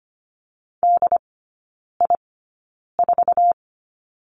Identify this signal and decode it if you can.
Morse code, heard as a keyed tone, about 25 wpm, spaced as for 9 wpm, 700 Hz. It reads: DI4
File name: Lesson 15e - Number 5 - Three Characters 25wpm